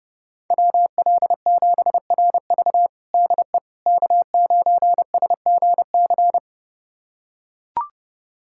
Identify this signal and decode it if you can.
Morse code, heard as a keyed tone, about 30 wpm, 700 Hz. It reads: WL7RV DE K9SGC
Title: USA Callsign DE Callsign - 5 Characters - 30wpm